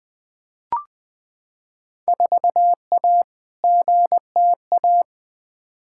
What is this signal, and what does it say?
Morse code, heard as a keyed tone, about 20 wpm, 700 Hz. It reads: 4A GTA
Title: Field Day Exchanges Encore - 20wpm